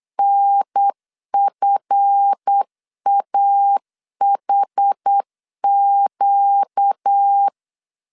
Morse code, heard as a keyed tone, about 8 wpm, 800 Hz. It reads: NFAHQ
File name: LEZIONI / 7 GF9Z / 7 GF9 8wpm 3545